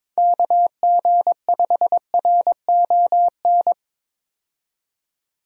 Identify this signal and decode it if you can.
Morse code, heard as a keyed tone, about 22 wpm, 700 Hz. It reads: KG5RON
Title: USA Call Signs - 6 Characters - 22wpm